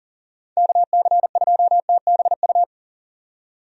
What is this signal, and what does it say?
Morse code, heard as a keyed tone, about 40 wpm, 700 Hz. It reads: KC2TBU
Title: USA Call Signs - 6 Characters - 40wpm